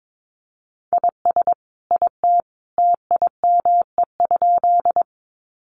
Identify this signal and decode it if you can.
Morse code, heard as a keyed tone, about 22 wpm, 700 Hz. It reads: IS IT TIME?
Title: Sentences from Top 100 Words 22wpm